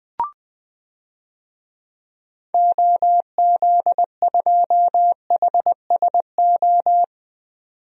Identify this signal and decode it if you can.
Morse code, heard as a keyed tone, about 20 wpm, 700 Hz. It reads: OZ2HSO
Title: DXCC-100 Call Signs - 6 Characters - 20wpm